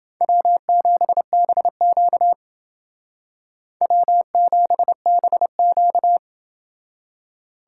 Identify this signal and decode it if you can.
Morse code, heard as a keyed tone, about 28 wpm, 700 Hz. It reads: W7BQ W7BQ